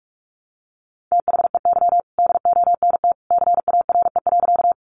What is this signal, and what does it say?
Morse code, heard as a keyed tone, about 45 wpm, 700 Hz. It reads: THEY DONT CARE.
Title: Sentences from Top 300 Words 45wpm